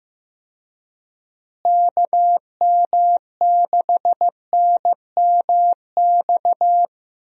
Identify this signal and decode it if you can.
Morse code, heard as a keyed tone, about 15 wpm, 700 Hz. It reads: KM6NMX
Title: USA Call Signs - 6 Characters - Repeated 2x - 15wpm